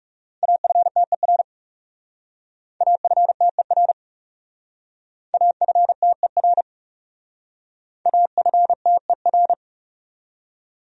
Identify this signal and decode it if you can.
Morse code, heard as a keyed tone, about 38 wpm, 700 Hz. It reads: AFTER AFTER AFTER AFTER